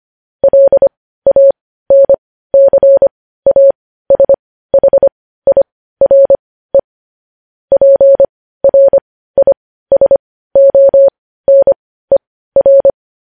Morse code, heard as a keyed tone, about 25 wpm, spaced as for 15 wpm, 550 Hz. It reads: LANCASHIRE PRISONER
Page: news headlines in morse code at 15 wpm